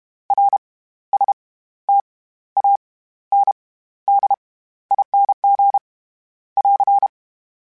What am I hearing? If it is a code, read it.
Morse code, heard as a keyed tone, about 32 wpm, spaced as for 15 wpm, 800 Hz. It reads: RSTANDING <AR>
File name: Special INGs / Special INGs-mp3s / SlowToFaster ing @30